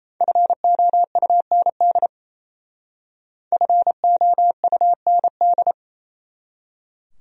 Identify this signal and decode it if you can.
Morse code, heard as a keyed tone, about 37 wpm, 700 Hz. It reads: FOUND FOUND